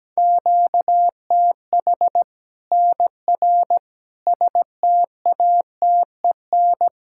Morse code, heard as a keyed tone, about 17 wpm, 700 Hz. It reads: QTH NR STATEN
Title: Practice QSO - Exchange 1 - Beginning - 17wpm